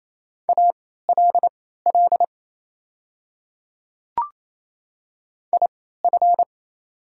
Morse code, heard as a keyed tone, about 28 wpm, spaced as for 16 wpm, 700 Hz. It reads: ALL IF
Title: 3x Character Spacing - Top 200 Words - 28wpm